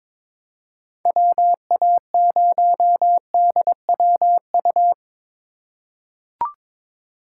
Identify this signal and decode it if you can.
Morse code, heard as a keyed tone, about 22 wpm, 700 Hz. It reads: WA0DWU
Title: USA Call Signs - 6 Characters - 3x Character Spacing - 22wpm